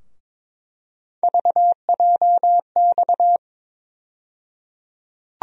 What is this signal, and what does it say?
Morse code, heard as a keyed tone, about 23 wpm, 700 Hz. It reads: VJX
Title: USA Call Sign Postfixes - 3 Letters 22wpm